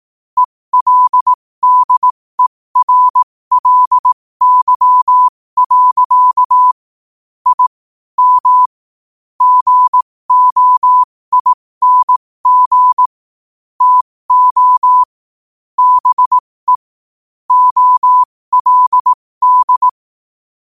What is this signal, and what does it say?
Morse code, heard as a keyed tone, about 18 wpm, spaced as for 15 wpm, 1000 Hz. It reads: ELDERLY. I M GOING TO BE OLD
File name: qotdfiles / 15wpm